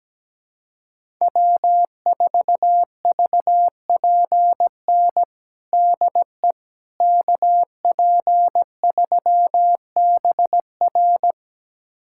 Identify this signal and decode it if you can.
Morse code, heard as a keyed tone, about 17 wpm, 700 Hz. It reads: W4VPN DE KP3BR
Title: USA Callsign DE Callsign - 5 Characters - 17wpm